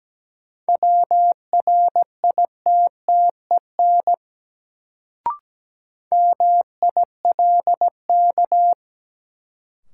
Morse code, heard as a keyed tone, about 17 wpm, 700 Hz. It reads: WRITTEN MILK
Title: Sentences from Top 1500 Words - New Word Drill - 17wpm